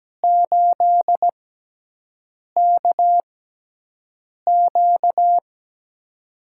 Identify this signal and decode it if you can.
Morse code, heard as a keyed tone, about 17 wpm, spaced as for 6 wpm, 700 Hz. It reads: 8KQ
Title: USA Call Signs - 5 Characters - 6x Character Spacing - 17wpm